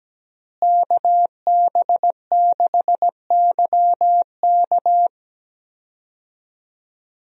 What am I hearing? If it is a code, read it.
Morse code, heard as a keyed tone, about 17 wpm, 700 Hz. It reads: KB6YK
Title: USA Call Signs - 5 Characters - Repeated 3x - 17wpm